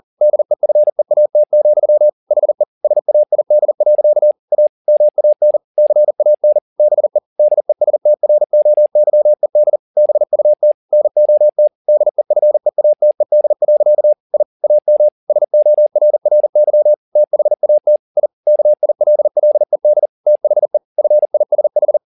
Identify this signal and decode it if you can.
Morse code, heard as a keyed tone, about 40 wpm, 600 Hz. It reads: DEFEAT, HE SAID. A MAN CAN BE DESTROYED BUT NOT DEFEATED. I AM SORRY THAT I KILLED THE FISH
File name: the old man and the sea 2 40wpm